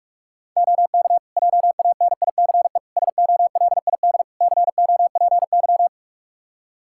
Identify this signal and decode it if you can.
Morse code, heard as a keyed tone, about 42 wpm, 700 Hz. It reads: OK JANICE SOLID COPY
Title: Practice QSO - Exchange 3 - More Info - 45wpm